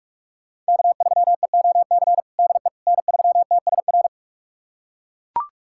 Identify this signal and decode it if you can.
Morse code, heard as a keyed tone, about 41 wpm, 700 Hz. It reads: K3EOC DE N3TSR
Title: USA Callsign DE Callsign - 5 Characters - 45wpm